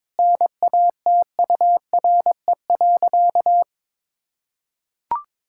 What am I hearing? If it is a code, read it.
Morse code, heard as a keyed tone, about 22 wpm, 700 Hz. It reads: NATURE.
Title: Sentences from Top 1000 Words - Review Part 6 - 22wpm